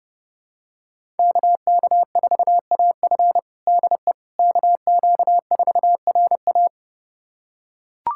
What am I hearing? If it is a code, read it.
Morse code, heard as a keyed tone, about 32 wpm, 700 Hz. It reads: KK4AF DE KQ4RA
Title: USA Callsign DE Callsign - 5 Characters - 30wpm